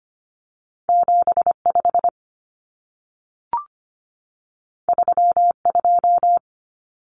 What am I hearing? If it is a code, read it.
Morse code, heard as a keyed tone, about 25 wpm, 700 Hz. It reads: 75 32
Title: Common Cut Numbers - 2-Digit 25wpm